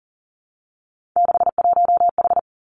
Morse code, heard as a keyed tone, about 38 wpm, 700 Hz. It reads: 61H
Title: Three Challenging Letter-Number Combinations 40wpm